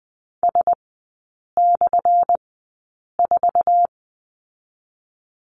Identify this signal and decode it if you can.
Morse code, heard as a keyed tone, about 19 wpm, 700 Hz. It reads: S / 4
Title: 198 Lesson 32e - Letter k - Three Characters 20wpm